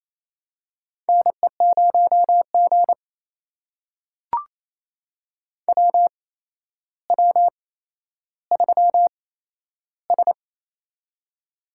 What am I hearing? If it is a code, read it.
Morse code, heard as a keyed tone, about 28 wpm, spaced as for 12 wpm, 700 Hz. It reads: NE0G WW3S
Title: USA Call Signs - 4 Characters - 8x Character Spacing - 28wpm